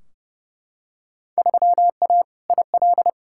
Morse code, heard as a keyed tone, about 31 wpm, 700 Hz. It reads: 3A IL
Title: Field Day Exchanges - 30wpm